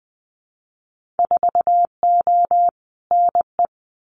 Morse code, heard as a keyed tone, about 20 wpm, 700 Hz. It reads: 4O NE
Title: Winter Field Day Exchanges Encore 20wpm